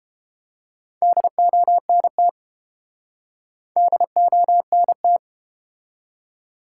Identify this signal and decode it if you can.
Morse code, heard as a keyed tone, about 31 wpm, 700 Hz. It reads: DONT DONT